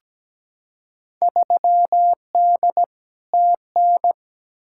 Morse code, heard as a keyed tone, about 17 wpm, 700 Hz. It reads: 3D TN
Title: Field Day Exchanges Encore - 17wpm